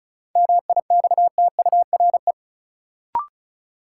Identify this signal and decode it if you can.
Morse code, heard as a keyed tone, about 36 wpm, 700 Hz. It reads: MIXTURE
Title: Sentences from Top 1500 Words - New Word Drill - 35wpm